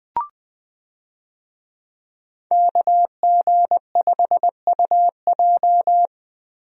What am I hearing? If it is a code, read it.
Morse code, heard as a keyed tone, about 20 wpm, 700 Hz. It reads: KG5UJ
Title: USA Call Signs - 5 Characters - Repeated 3x - 20wpm